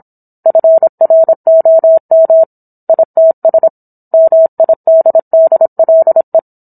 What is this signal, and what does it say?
Morse code, heard as a keyed tone, about 27 wpm, 650 Hz. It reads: FROM ITS MIDDLE